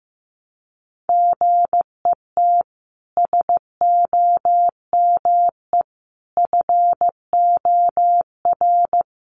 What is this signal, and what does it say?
Morse code, heard as a keyed tone, about 15 wpm, 700 Hz. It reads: GET SOME FOR
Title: Sentences from Top 1300 Words 15wpm